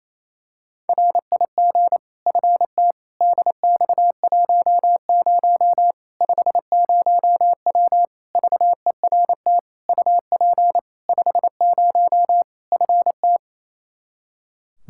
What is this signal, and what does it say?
Morse code, heard as a keyed tone, about 28 wpm, 700 Hz. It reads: RIG FT DX10 50W VERT UP 50 FT
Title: Practice QSO - Exchange 2 - Info - 28wpm